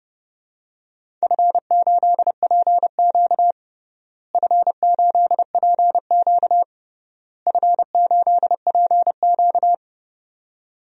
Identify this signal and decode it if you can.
Morse code, heard as a keyed tone, about 30 wpm, 700 Hz. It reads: F8PQ F8PQ F8PQ
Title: DXCC-100 Call Signs - 4 Characters - Repeated 3x - 30wpm